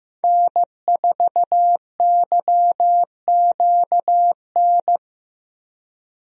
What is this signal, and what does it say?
Morse code, heard as a keyed tone, about 15 wpm, 700 Hz. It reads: N4YQN
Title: USA Call Signs - 5 Characters - Repeated 3x - 15wpm